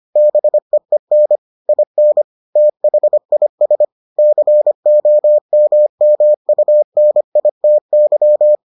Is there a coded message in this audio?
Yes: Morse code from 0.2 to 8.6 s, heard as a keyed tone, about 25 wpm, 600 Hz. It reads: BEEN IN THIS COMMUNITY